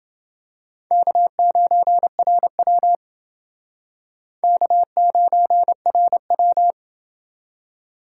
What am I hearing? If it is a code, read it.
Morse code, heard as a keyed tone, about 29 wpm, 700 Hz. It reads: K9RW K9RW